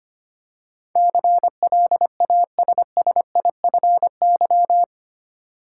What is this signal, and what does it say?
Morse code, heard as a keyed tone, about 25 wpm, 700 Hz. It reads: CLASSIFY
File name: 8 Letter Words - First 3 Letters Given - 25wpm